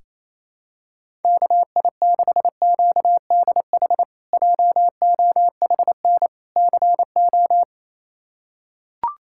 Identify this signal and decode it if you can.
Morse code, heard as a keyed tone, about 27 wpm, 700 Hz. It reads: KI6QDH JOHN CO
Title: K1USN SST QSO Practice - 28wpm